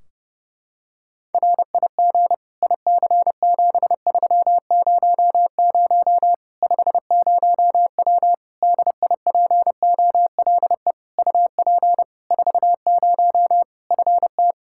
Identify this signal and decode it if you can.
Morse code, heard as a keyed tone, about 30 wpm, 700 Hz. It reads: RIG IC7300 50W DIPOLE UP 40 FT
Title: Practice QSO - Exchange 2 - Info - 30wpm